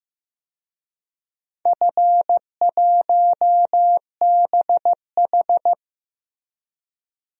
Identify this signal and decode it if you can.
Morse code, heard as a keyed tone, about 16 wpm, 700 Hz. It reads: F1BH
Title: DXCC-100 Call Signs - 4 Characters - Repeated 2x - 15wpm